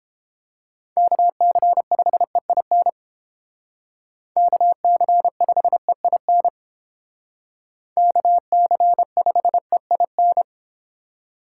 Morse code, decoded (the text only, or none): KC5EIN KC5EIN KC5EIN